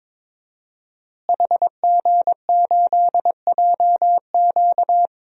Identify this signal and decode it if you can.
Morse code, heard as a keyed tone, about 22 wpm, 700 Hz. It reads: HG8JQ
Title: DXCC-100 Call Signs - 5 Characters - Repeated 3x - 22wpm